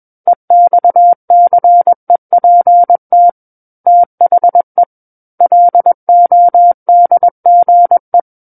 Morse code, heard as a keyed tone, about 21 wpm, 700 Hz. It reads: EXCEPT THE LODGE
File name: Book One / 12-Chapter 12 - 21 WPM